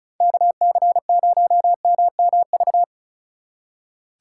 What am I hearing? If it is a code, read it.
Morse code, heard as a keyed tone, about 45 wpm, 700 Hz. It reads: KC0MMV